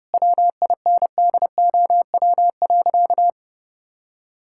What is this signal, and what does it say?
Morse code, heard as a keyed tone, about 32 wpm, 700 Hz. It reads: WINDOW.
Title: Sentences from Top 1600 Words 30wpm